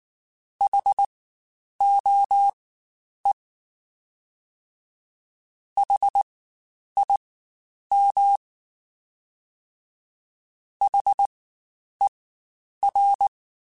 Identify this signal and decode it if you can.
Morse code, heard as a keyed tone, about 19 wpm, spaced as for 9 wpm, 800 Hz. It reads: HOE HIM HER
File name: Spaced Alphas / Spaced Alphas mp3s / Hs sp @18